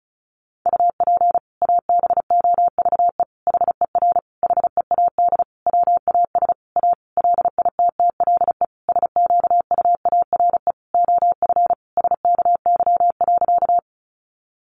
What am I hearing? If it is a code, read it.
Morse code, heard as a keyed tone, about 34 wpm, 700 Hz. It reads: UP ABOVE HER HEAD WAS A LITTLE SQUARE OF SKY.
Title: Sentences from Top 1200 Words 35wpm